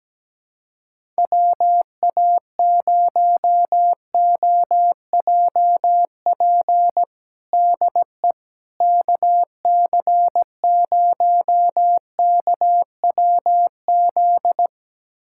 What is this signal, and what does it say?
Morse code, heard as a keyed tone, about 17 wpm, 700 Hz. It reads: WA0OJP DE KC0KWZ
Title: USA Callsign DE Callsign - 6 Characters - 17wpm